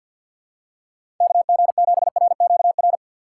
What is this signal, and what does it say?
Morse code, heard as a keyed tone, about 53 wpm, 700 Hz. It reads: KG7RQR
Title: USA Call Signs - 6 Characters - 8x Character Spacing - 50wpm